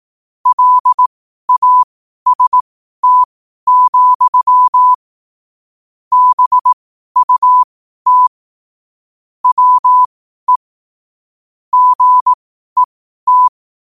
Morse code, heard as a keyed tone, about 18 wpm, spaced as for 12 wpm, 1000 Hz. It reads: LAST, BUT WE GET
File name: qotdfiles / 12wpm